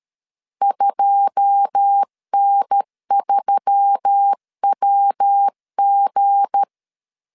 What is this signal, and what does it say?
Morse code, heard as a keyed tone, about 13 wpm, 800 Hz. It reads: 2N3WG
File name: LEZIONI / 9 2 12wpm 2301